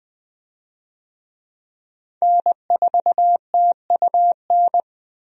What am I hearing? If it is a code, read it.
Morse code, heard as a keyed tone, about 20 wpm, 700 Hz. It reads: N4TUN
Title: USA Call Signs - 5 Characters - 20wpm